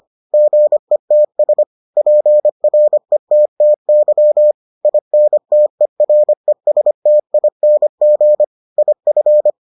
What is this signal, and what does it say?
Morse code, heard as a keyed tone, about 25 wpm, 600 Hz. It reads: GETS PRETTY INTERESTING IF